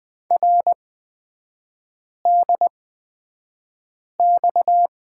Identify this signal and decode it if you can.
Morse code, heard as a keyed tone, about 20 wpm, spaced as for 5 wpm, 700 Hz. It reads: RDX
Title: 255 Lesson 41e - Period - Three Characters 20wpm